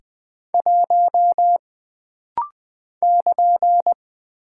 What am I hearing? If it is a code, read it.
Morse code, heard as a keyed tone, about 20 wpm, 700 Hz. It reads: 1 <KN>
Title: Single Character (Letter:Number:Punctuation:Prosign) - Rapid-Fire - 20wpm